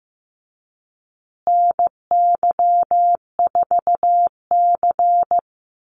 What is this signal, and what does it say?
Morse code, heard as a keyed tone, about 15 wpm, 700 Hz. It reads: NY4C
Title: USA Call Signs Repeated 3x 15wpm